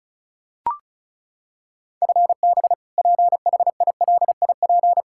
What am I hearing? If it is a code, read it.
Morse code, heard as a keyed tone, about 34 wpm, 700 Hz. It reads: FB PHILIP
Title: Practice QSO - Exchange 3 - More Info - 35wpm